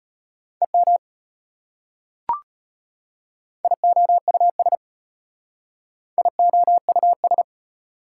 Morse code, heard as a keyed tone, about 30 wpm, 700 Hz. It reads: EM IOUS IOUS
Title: Speed-Racing Prefixes and Suffixes 25wpm